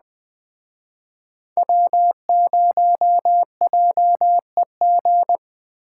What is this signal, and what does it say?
Morse code, heard as a keyed tone, about 20 wpm, 700 Hz. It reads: W0JEG
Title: Speed Racing - USA 1x3 Call Signs 17wpm